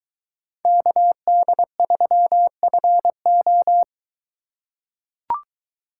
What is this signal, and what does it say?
Morse code, heard as a keyed tone, about 21 wpm, 700 Hz. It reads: KD3FO